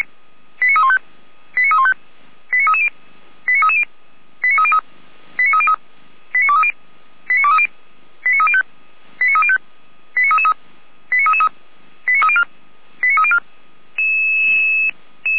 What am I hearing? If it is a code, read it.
Five-tone selective call (ZVEI), heard as quick runs of five tones, tones of about 70 ms. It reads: 89315 89315 89330 89330 89303 89303 89230 89230 89305 89305 89333 89333 89304 89304
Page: SDR/ZVEI